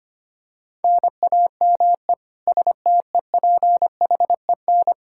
Morse code, heard as a keyed tone, about 25 wpm, 700 Hz. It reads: NAME STEPHEN
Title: Practice QSO - Exchange 1 - Beginning - 25wpm